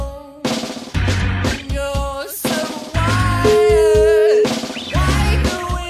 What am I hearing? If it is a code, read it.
Ringback tone (Europe, 425 Hz), with noise.